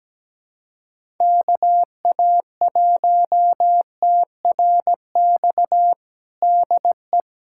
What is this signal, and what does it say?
Morse code, heard as a keyed tone, about 17 wpm, 700 Hz. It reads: KA1TRX DE
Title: USA Callsign DE Callsign - 6 Characters - 17wpm